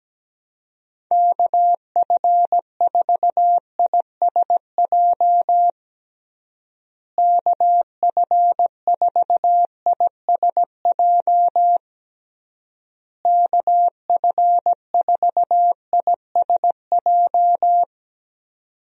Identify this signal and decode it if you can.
Morse code, heard as a keyed tone, about 17 wpm, 700 Hz. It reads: KF4ISJ KF4ISJ KF4ISJ
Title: USA Call Signs - 6 Characters - Repeated 3x - 17wpm